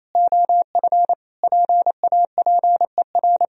Morse code, heard as a keyed tone, about 28 wpm, 700 Hz. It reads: OF PAPER
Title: Sentences from Top 1000 Words - Review Part 2 - 28wpm